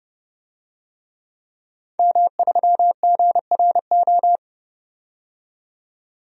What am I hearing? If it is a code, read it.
Morse code, heard as a keyed tone, about 30 wpm, 700 Hz. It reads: M3GRO
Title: DXCC-100 Call Signs - 5 Characters - 30wpm